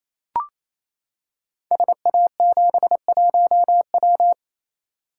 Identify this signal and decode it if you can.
Morse code, heard as a keyed tone, about 28 wpm, 700 Hz. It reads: SA71W
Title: Code Groups - 5 Letter:Numbers - 28wpm